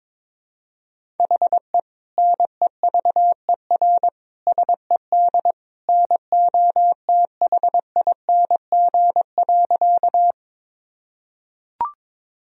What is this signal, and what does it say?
Morse code, heard as a keyed tone, about 22 wpm, 700 Hz. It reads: HE NEVER SED NOTHING.
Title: Sentences from Top 500 Words and Abbreviations 22wpm